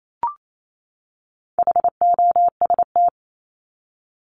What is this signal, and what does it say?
Morse code, heard as a keyed tone, about 28 wpm, 700 Hz. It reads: HOST
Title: Top 1501-1750 Words 28wpm